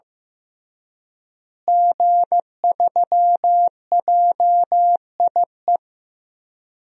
Morse code, heard as a keyed tone, about 15 wpm, 700 Hz. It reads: G3JIE
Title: DXCC-100 Call Signs - 5 Characters - Repeated 3x - 15wpm